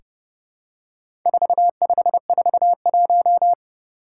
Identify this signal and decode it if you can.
Morse code, heard as a keyed tone, about 30 wpm, 700 Hz. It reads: 4541
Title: Four Challenging Letter-Number Combinations 30wpm